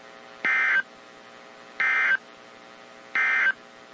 Emergency Alert System (SAME) tones, heard as harsh buzzing bursts. It is heard from 0.3 to 3.6 s.
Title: WZ2523 - NOAA Weather Radio